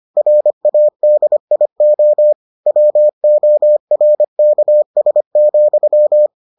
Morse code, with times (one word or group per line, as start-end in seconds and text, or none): 0.2-2.3 RADIO
2.7-6.3 WORKS,